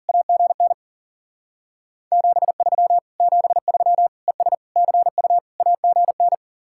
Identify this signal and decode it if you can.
Morse code, heard as a keyed tone, about 49 wpm, 700 Hz. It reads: AGN 73 73 ES CU AGN
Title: Speed-Racing Common QSO Phrases - 40wpm